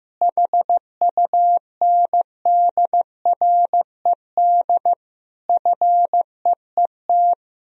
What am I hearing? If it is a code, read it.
Morse code, heard as a keyed tone, about 15 wpm, 700 Hz. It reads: HUNDRED FEET
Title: Sentences from Top 1000 Words - Review Part 2 - 15wpm